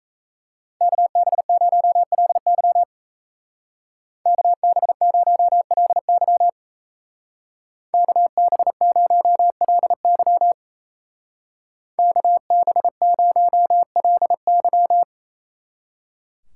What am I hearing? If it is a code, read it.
Morse code, heard as a keyed tone, about 35 wpm, 700 Hz. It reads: KB0LY KB0LY KB0LY KB0LY
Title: Speed Racing - USA 2x2 Call Signs 28wpm